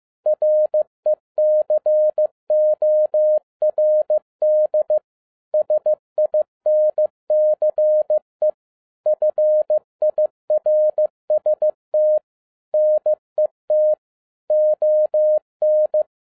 Morse code, heard as a keyed tone, about 15 wpm, 610 Hz. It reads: RECORD SINCE FIRST NET ON
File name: 2012-01-mp3 / n17-SUMMARY OF FEA NET IN 2011-15wpm